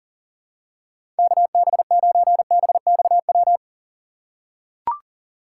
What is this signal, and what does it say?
Morse code, heard as a keyed tone, about 40 wpm, 700 Hz. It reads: KB9BXW
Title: USA Call Signs - 6 Characters - 2x Character Spacing - 40wpm